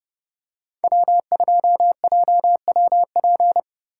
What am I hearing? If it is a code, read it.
Morse code, heard as a keyed tone, about 30 wpm, 700 Hz. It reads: W2JWP
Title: USA Call Signs - 4x Character Spacing - 30wpm